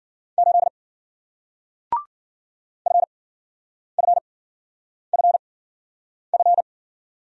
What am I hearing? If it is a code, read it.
Morse code, heard as a keyed tone, about 56 wpm, spaced as for 9 wpm, 700 Hz. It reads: 8 FFFF